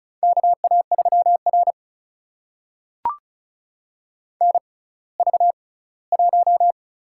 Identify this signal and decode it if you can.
Morse code, heard as a keyed tone, about 34 wpm, spaced as for 20 wpm, 700 Hz. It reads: KA3R NV1
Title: USA Call Signs - 4 Characters - 6x Character Spacing - 35wpm